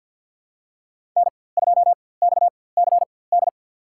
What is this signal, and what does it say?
Morse code, heard as a keyed tone, about 52 wpm, spaced as for 23 wpm, 700 Hz. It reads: N2XCD
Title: USA Call Signs - 5 Characters - 4x Character Spacing - 50wpm